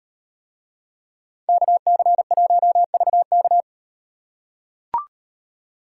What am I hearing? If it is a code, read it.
Morse code, heard as a keyed tone, about 38 wpm, 700 Hz. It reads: KC1VK